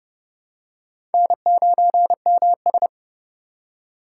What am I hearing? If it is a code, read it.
Morse code, heard as a keyed tone, about 30 wpm, 700 Hz. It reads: N9MS
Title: K1USN SST QSO Practice - 30wpm